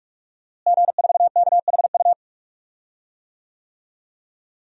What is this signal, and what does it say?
Morse code, heard as a keyed tone, about 41 wpm, 700 Hz. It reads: G4KHU
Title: DXCC-100 Call Signs - 5 Characters - Repeated 3x - 45wpm